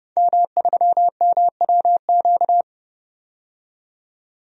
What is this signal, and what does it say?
Morse code, heard as a keyed tone, about 32 wpm, 700 Hz. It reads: M3MWQ
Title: DXCC-100 Call Signs - 5 Characters - 30wpm